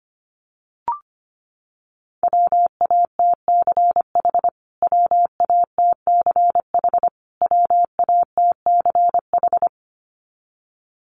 Morse code, heard as a keyed tone, about 25 wpm, 700 Hz. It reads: WATCH WATCH WATCH
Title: Top 300 Words Repeated 25wpm